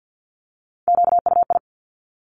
Morse code, heard as a keyed tone, about 50 wpm, 700 Hz. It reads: QUI
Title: Top Three Letter Combinations in Words 50wpm